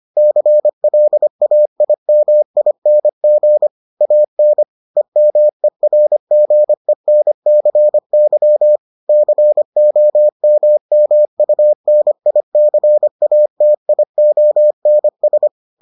Morse code, heard as a keyed tone, about 25 wpm, 600 Hz. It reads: CLAIMING AN EMERGENCY COMMUNICATIONS